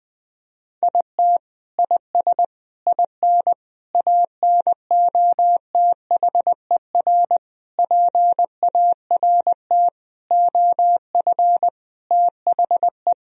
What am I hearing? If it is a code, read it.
Morse code, heard as a keyed tone, about 20 wpm, 700 Hz. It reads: IT IS IN ANOTHER PART OF THE
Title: Sentences from Top 1000 Words - Review Part 2 - 20wpm